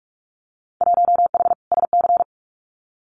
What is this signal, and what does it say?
Morse code, heard as a keyed tone, about 44 wpm, 700 Hz. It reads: 1H SC
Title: Winter Field Day Exchanges 45wpm